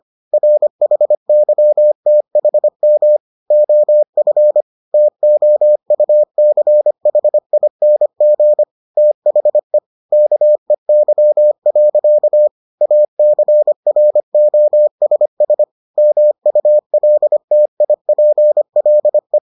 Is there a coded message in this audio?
Yes: Morse code from 0.3 to 19.4 s, heard as a keyed tone, about 25 wpm, 600 Hz. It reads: RHYTHM OF TOUCHING THE KEY. ACROSS MULTIPLE